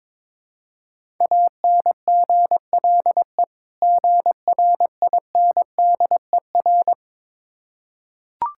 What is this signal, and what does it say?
Morse code, heard as a keyed tone, about 22 wpm, 690 Hz. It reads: ANGLE GRINDER
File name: Difficult Words with Hints - 22wpm